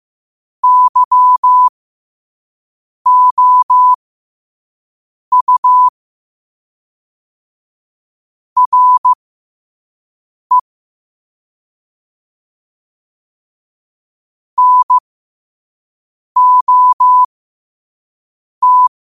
Morse code, heard as a keyed tone, about 15 wpm, spaced as for 5 wpm, 1000 Hz. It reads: YOU RE NOT
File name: qotdfiles / 5wpm